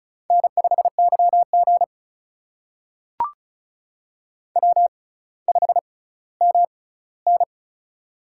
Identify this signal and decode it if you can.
Morse code, heard as a keyed tone, about 36 wpm, spaced as for 18 wpm, 700 Hz. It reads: N5YG W5MN
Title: USA Call Signs - 4 Characters - 6x Character Spacing - 35wpm